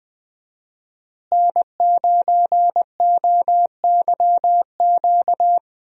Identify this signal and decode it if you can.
Morse code, heard as a keyed tone, about 20 wpm, 700 Hz. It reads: N9OYQ
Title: USA Call Signs - 5 Characters - Repeated 3x - 20wpm